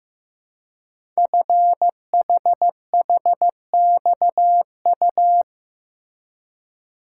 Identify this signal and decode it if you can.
Morse code, heard as a keyed tone, about 15 wpm, 700 Hz. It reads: FHHXU
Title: Code Groups - 5 Letters - 15wpm